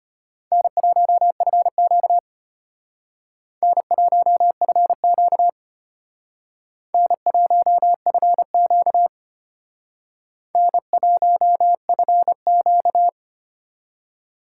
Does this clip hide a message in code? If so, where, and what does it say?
Yes: Morse code from 0.5 to 13.1 s, heard as a keyed tone, about 32 wpm, 700 Hz. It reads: N1FQ N1FQ N1FQ N1FQ